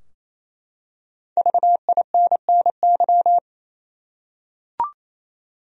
Morse code, heard as a keyed tone, about 28 wpm, 700 Hz. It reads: VINNY
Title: USA Common Names - Max 5 Letters - 28wpm